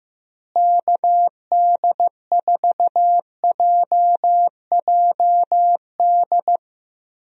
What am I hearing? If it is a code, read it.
Morse code, heard as a keyed tone, about 16 wpm, 700 Hz. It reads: KD4JJD
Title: USA Call Signs - 6 Characters - Repeated 2x - 15wpm